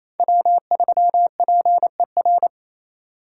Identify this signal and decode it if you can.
Morse code, heard as a keyed tone, about 35 wpm, 700 Hz. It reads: W3PER
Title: Speed Racing - USA 1x3 Call Signs 28wpm